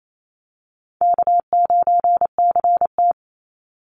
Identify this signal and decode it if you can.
Morse code, heard as a keyed tone, about 28 wpm, 700 Hz. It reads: K9CT
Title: Call Signs Repeated 3x - 28wpm